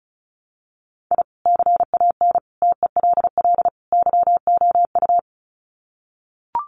Morse code, heard as a keyed tone, about 34 wpm, 700 Hz. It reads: I CAN TELL YOU
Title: Sets of 4 Words 35wpm